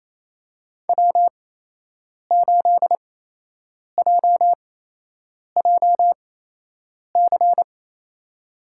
Morse code, heard as a keyed tone, about 27 wpm, spaced as for 8 wpm, 700 Hz. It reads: W8JJC
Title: USA Call Signs - 5 Characters - 8x Character Spacing - 28wpm